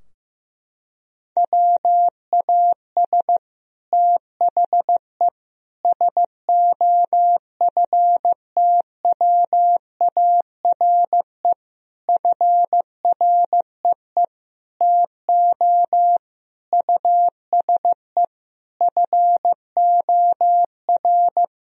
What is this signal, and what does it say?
Morse code, heard as a keyed tone, about 16 wpm, 700 Hz. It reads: WAS THE SOFTWARE FREE TO USE FOR
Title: Sentences from Top 1000 Words - Review Part 3 - 15wpm